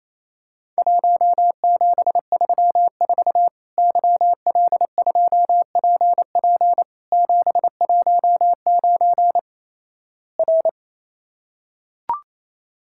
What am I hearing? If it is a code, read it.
Morse code, heard as a keyed tone, about 28 wpm, 700 Hz. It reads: 1734 YL2PP 719 R
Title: Worked All Europe DX Contest - QTC Exchange - 28wpm